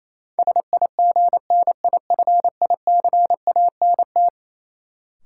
Morse code, heard as a keyed tone, about 28 wpm, 700 Hz. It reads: SIGNIFICANT
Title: Sentences from Top 1000 Words - Review Part 6 - New Word Drill - 28wpm